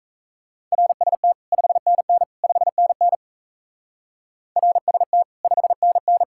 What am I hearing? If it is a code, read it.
Morse code, heard as a keyed tone, about 34 wpm, 700 Hz. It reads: RST 5NN 5NN RST 5NN
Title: Speed-Racing Common QSO Phrases Encore - 28wpm